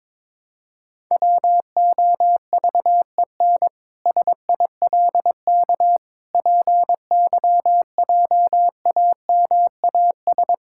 Morse code, heard as a keyed tone, about 22 wpm, 700 Hz. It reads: WOVEN SILK PYJAMAS
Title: Pangram Challenge v2 - 22wpm